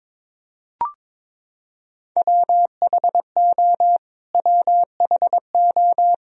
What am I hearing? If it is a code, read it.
Morse code, heard as a keyed tone, about 22 wpm, 700 Hz. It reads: WHO WHO
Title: Top 100 Words Repeated 22wpm